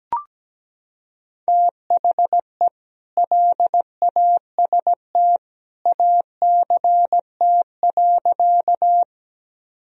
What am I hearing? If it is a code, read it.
Morse code, heard as a keyed tone, about 17 wpm, 700 Hz. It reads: THE LAST ACT.
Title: Sentences from Top 1000 Words - Review Part 1 - 17wpm